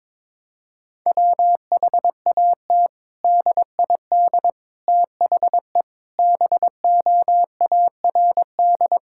Morse code, heard as a keyed tone, about 22 wpm, 700 Hz. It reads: WHAT DID THE BOARD
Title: Sentences from Top 1000 Words - Review Part 6 - 22wpm